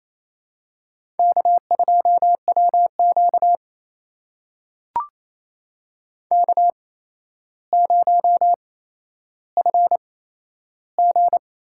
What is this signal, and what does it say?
Morse code, heard as a keyed tone, about 28 wpm, spaced as for 12 wpm, 700 Hz. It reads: K2WQ K0FG
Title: USA Call Signs - 8x Character Spacing - 28wpm